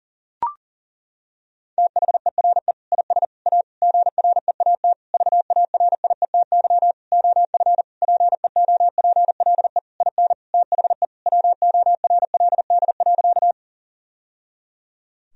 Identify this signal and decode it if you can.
Morse code, heard as a keyed tone, about 37 wpm, 700 Hz. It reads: THERE IS A GREAT VARIETY OF PEOPLE IN THE WORLD.
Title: Sentences from Top 1000 Words - Review Part 4 - 40wpm